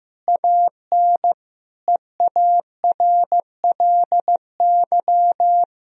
Morse code, heard as a keyed tone, about 16 wpm, 700 Hz. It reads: AN EARLY
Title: Sentences from Top 1000 Words - Review Part 2 - 15wpm